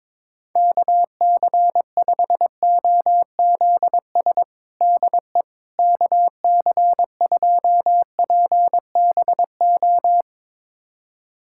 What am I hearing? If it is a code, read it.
Morse code, heard as a keyed tone, about 22 wpm, 700 Hz. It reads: KC5OZS DE KC2PBO
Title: USA Callsign DE Callsign 22wpm